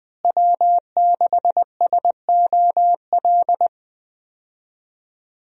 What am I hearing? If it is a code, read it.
Morse code, heard as a keyed tone, about 20 wpm, 700 Hz. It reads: W6SOL
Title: USA Call Signs - 5 Characters - Repeated 3x - 20wpm